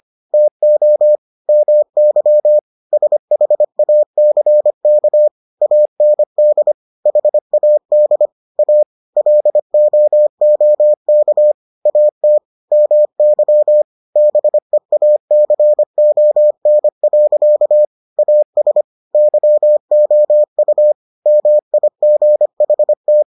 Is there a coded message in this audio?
Yes: Morse code from 0.3 to 23.2 s, heard as a keyed tone, about 25 wpm, 600 Hz. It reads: TO MY SHACK AND HAD A LOOK AT MY BEACON. AS YOU MIGHT